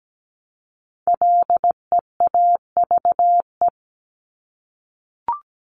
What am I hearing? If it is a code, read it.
Morse code, heard as a keyed tone, about 17 wpm, 700 Hz. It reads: LEAVE
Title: Top 500 Words Repeated 17wpm